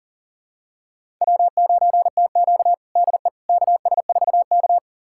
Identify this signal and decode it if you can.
Morse code, heard as a keyed tone, about 40 wpm, 700 Hz. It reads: W9TQ DE KS4K
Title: USA Callsign DE Callsign - 4 Characters - 40wpm